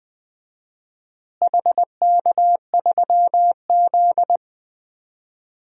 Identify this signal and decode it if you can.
Morse code, heard as a keyed tone, about 21 wpm, 700 Hz. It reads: HK3Z
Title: DXCC-100 Call Signs - 4 Characters - Repeated 3x - 20wpm